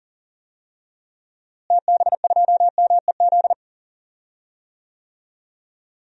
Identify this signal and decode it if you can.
Morse code, heard as a keyed tone, about 40 wpm, 700 Hz. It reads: TB2MEZ
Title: DXCC-100 Call Signs - 6 Characters - Repeated 3x - 40wpm